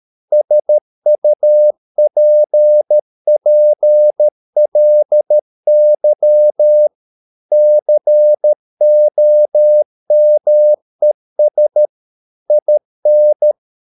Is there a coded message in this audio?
Yes: Morse code from 0.3 to 13.5 s, heard as a keyed tone, about 13 wpm, 600 Hz. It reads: SUPPLY COMES IN